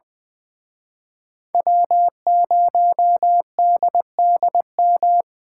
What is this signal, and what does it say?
Morse code, heard as a keyed tone, about 24 wpm, 700 Hz. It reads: W0DDM
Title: Speed Racing - USA 1x3 Call Signs 20wpm